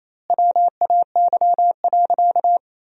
Morse code, heard as a keyed tone, about 28 wpm, 700 Hz. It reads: WAY.
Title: 2x Word-Spacing - Sentences from Top 200 Words - 28wpm